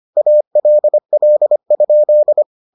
Morse code, heard as a keyed tone, about 25 wpm, 600 Hz. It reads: ALL?